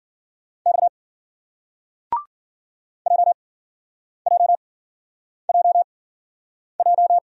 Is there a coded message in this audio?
Yes: Morse code from 0.7 to 7.2 s, heard as a keyed tone, about 54 wpm, spaced as for 9 wpm, 700 Hz. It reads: X JJJJ